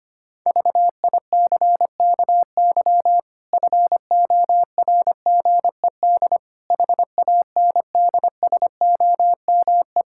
Morse code, heard as a keyed tone, about 25 wpm, 700 Hz. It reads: VICKY FORGED HANDSOME
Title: Pangram Challenge v2 - 25wpm